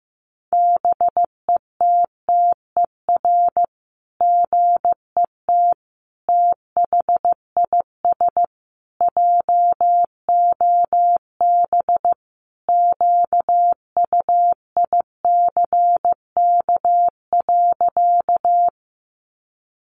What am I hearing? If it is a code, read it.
Morse code, heard as a keyed tone, about 15 wpm, 700 Hz. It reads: BETTER GET THIS JOB QUICK.
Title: Sentences from Top 600 Words 15wpm